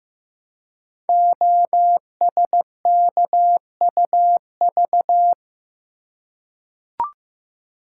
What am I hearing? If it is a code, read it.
Morse code, heard as a keyed tone, about 15 wpm, 700 Hz. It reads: OSKUV
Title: Code Groups - 5 Letters - 15wpm